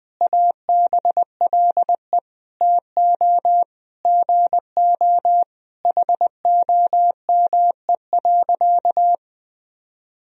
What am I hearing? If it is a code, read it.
Morse code, heard as a keyed tone, about 20 wpm, 700 Hz. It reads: ABLE TO GO HOME.
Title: Sentences from Top 1000 Words - Review Part 3 - 20wpm